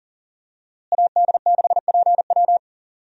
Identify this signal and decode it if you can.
Morse code, heard as a keyed tone, about 40 wpm, 700 Hz. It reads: AD6PW
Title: USA Call Signs - 5 Characters - Repeated 2x - 40wpm